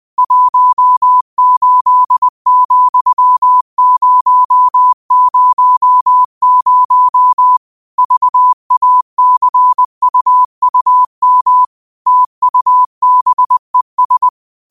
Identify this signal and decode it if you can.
Morse code, heard as a keyed tone, about 20 wpm, 1000 Hz. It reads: 18,000 VACUUM TUBES